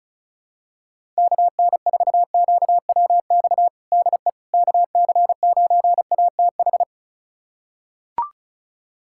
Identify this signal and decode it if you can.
Morse code, heard as a keyed tone, about 34 wpm, 700 Hz. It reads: KN4QWX DE KC9ATH
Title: USA Callsign DE Callsign - 6 Characters - 35wpm